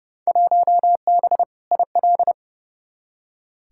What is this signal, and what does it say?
Morse code, heard as a keyed tone, about 32 wpm, 700 Hz. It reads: 1B IL
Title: Field Day Exchanges - 30wpm